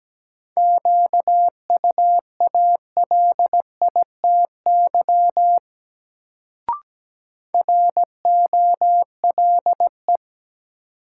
Morse code, heard as a keyed tone, about 17 wpm, 700 Hz. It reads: QUALITY ROLE
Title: Sentences from Top 1500 Words - New Word Drill Alt - 17wpm